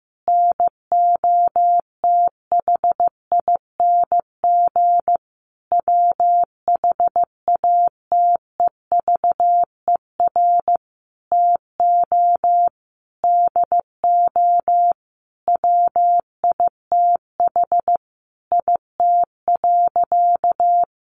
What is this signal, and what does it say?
Morse code, heard as a keyed tone, about 14 wpm, 700 Hz. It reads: NOTHING WHATEVER TO DO WITH IT.
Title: Sentences from Top 1100 Words 15wpm